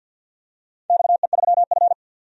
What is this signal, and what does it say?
Morse code, heard as a keyed tone, about 48 wpm, 700 Hz. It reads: XE3F
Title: DXCC-100 Call Signs - 4 Characters - 50wpm